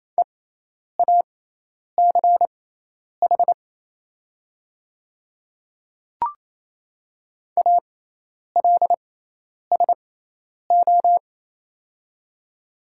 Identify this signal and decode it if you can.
Morse code, heard as a keyed tone, about 27 wpm, spaced as for 10 wpm, 700 Hz. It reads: EACH ALSO
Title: 6x Character Spacing - Top 200 Words - 28wpm